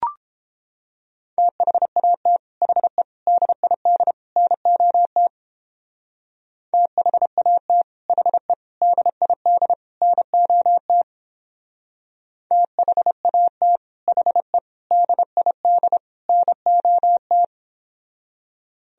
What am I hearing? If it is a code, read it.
Morse code, heard as a keyed tone, about 29 wpm, 700 Hz. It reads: THAT HE DID NOT THAT HE DID NOT THAT HE DID NOT